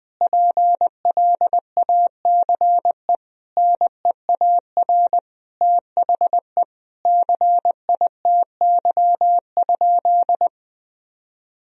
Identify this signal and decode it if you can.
Morse code, heard as a keyed tone, about 20 wpm, 700 Hz. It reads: PLACE NEAR THE CITY?
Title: Sentences from Top 1000 Words - Review Part 3 - 20wpm